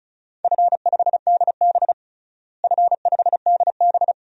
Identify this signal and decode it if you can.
Morse code, heard as a keyed tone, about 34 wpm, 700 Hz. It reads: F5DB F5DB
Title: DXCC-100 Call Signs - 4 Characters - Repeated 2x - 35wpm